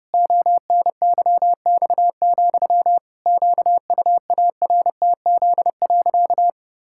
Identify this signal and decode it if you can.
Morse code, heard as a keyed tone, about 30 wpm, 700 Hz. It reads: ONYX, QUARTZ.
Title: Pangram Challenge v2 - 30wpm